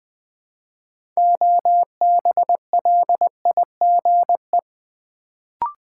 Morse code, heard as a keyed tone, about 20 wpm, 700 Hz. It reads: OBLIGE
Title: Build A Word - 3 Letters At A Time - 6-Letter Words - 20wpm